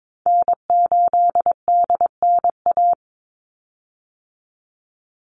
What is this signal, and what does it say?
Morse code, heard as a keyed tone, about 22 wpm, 700 Hz. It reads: N8DNA
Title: Call Signs Repeated 3x - 22wpm